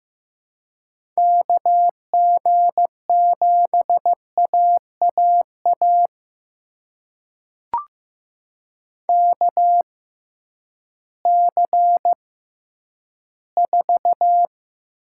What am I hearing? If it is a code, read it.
Morse code, heard as a keyed tone, about 15 wpm, spaced as for 10 wpm, 700 Hz. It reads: KG7AAA KC4
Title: USA Call Signs - 6 Characters - 6x Character Spacing - 15wpm